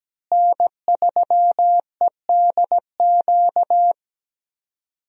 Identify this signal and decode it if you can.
Morse code, heard as a keyed tone, about 17 wpm, 700 Hz. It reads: N3EDQ
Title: USA Call Signs - 5 Characters - Repeated 2x - 17wpm